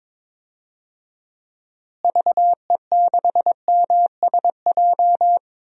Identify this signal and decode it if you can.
Morse code, heard as a keyed tone, about 22 wpm, 700 Hz. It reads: VE6MSJ
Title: DXCC-100 Call Signs - 6 Characters - Repeated 3x - 22wpm